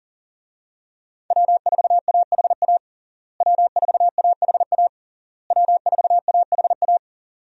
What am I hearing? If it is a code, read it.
Morse code, heard as a keyed tone, about 40 wpm, 700 Hz. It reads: W4AHA W4AHA W4AHA
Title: USA Call Signs - 5 Characters - Repeated 3x - 40wpm